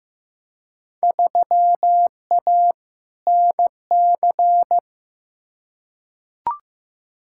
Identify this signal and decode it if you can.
Morse code, heard as a keyed tone, about 15 wpm, 700 Hz. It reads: 3A NC
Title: Field Day Exchanges Encore - 15wpm